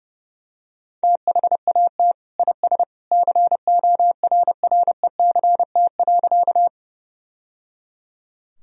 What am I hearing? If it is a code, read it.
Morse code, heard as a keyed tone, about 30 wpm, 700 Hz. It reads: THAT IS CORRECT.
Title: Sentences from Top 1000 Words - Review Part 2 - 30wpm